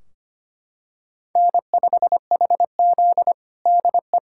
Morse code, heard as a keyed tone, about 25 wpm, 700 Hz. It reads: N5HZ DE
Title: Practice QSO - Exchange 4 - Ending - 25wpm